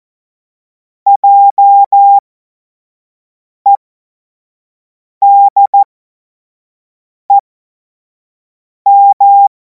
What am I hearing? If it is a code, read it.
Morse code, heard as a keyed tone, about 14 wpm, spaced as for 5 wpm, 800 Hz. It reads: JEDEM
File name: Morsekurs / Witze / Witz 07. April 2026 25 F70